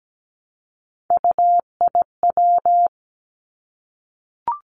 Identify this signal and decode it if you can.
Morse code, heard as a keyed tone, about 17 wpm, 700 Hz. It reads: UIW
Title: Three Challenging Letter-Number Combinations 17wpm